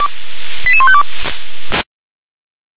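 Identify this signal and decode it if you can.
Five-tone selective call (ZVEI), heard as quick runs of five tones, tones of about 70 ms. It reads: 88152
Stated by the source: ZVEI five-tone call 88914 received on SDR.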